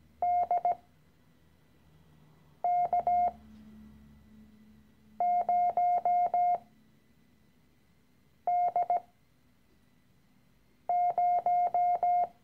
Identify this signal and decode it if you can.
Morse code, heard as a keyed tone, about 17 wpm, spaced as for 4 wpm, 710 Hz. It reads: DK0D0
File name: CW-Online-Kurs / Anfänger / Lektion 013